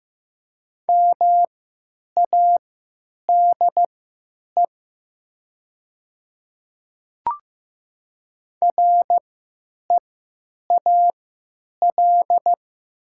Morse code, heard as a keyed tone, about 16 wpm, spaced as for 9 wpm, 700 Hz. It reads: MADE REAL
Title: 3x Character Spacing - Top 300 Words - 15wpm